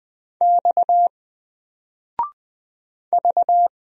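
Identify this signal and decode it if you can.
Morse code, heard as a keyed tone, about 20 wpm, 700 Hz. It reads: X V
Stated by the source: Single letter and number Morse drill at 20 wpm, from N4LCD.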